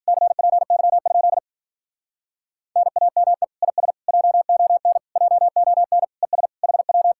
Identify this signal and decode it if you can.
Morse code, heard as a keyed tone, about 47 wpm, 700 Hz. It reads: CPY? NAME IS JON JON ES HW
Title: Speed-Racing Common QSO Phrases Encore - 40wpm